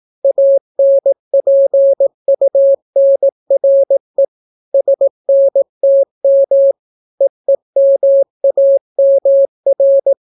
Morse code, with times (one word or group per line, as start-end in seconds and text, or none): 0.2-4.2 ANPUNRE
4.7-6.7 SNTM
7.2-10.1 EEMAMR